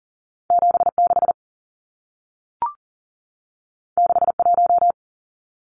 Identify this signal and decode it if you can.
Morse code, heard as a keyed tone, about 40 wpm, 700 Hz. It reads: 76 61
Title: Numbers - 2-Digit 40wpm